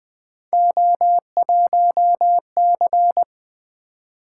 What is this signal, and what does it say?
Morse code, heard as a keyed tone, about 21 wpm, 700 Hz. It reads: O1C
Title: Three Letter-Number Combinations - 20wpm